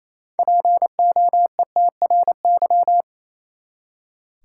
Morse code, heard as a keyed tone, about 28 wpm, 700 Hz. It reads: POETRY
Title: Six Letter Words - 28wpm